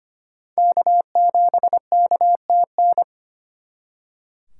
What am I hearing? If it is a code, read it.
Morse code, heard as a keyed tone, about 32 wpm, 700 Hz. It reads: K7KTN